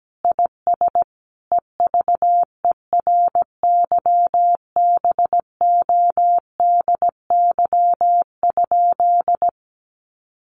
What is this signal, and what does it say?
Morse code, heard as a keyed tone, about 17 wpm, 700 Hz. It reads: IS EVERYBODY?
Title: Sentences from Top 1400 Words 17wpm